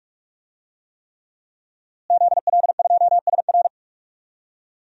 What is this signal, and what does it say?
Morse code, heard as a keyed tone, about 48 wpm, 700 Hz. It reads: ZL2SR
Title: DXCC-100 Call Signs - 5 Characters - Repeated 3x - 45wpm